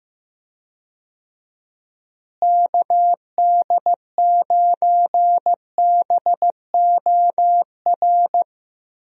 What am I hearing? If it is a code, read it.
Morse code, heard as a keyed tone, about 15 wpm, 700 Hz. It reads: KD9BOR
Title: USA Call Signs - 6 Characters - Repeated 3x - 15wpm